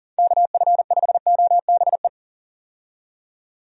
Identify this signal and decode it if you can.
Morse code, heard as a keyed tone, about 40 wpm, 700 Hz. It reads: KF5OBE
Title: USA Call Signs - 6 Characters - 40wpm